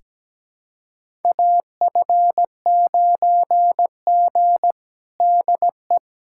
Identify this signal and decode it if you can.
Morse code, heard as a keyed tone, about 17 wpm, 700 Hz. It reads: AF9G DE
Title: USA Callsign DE Callsign - 4 Characters - 17wpm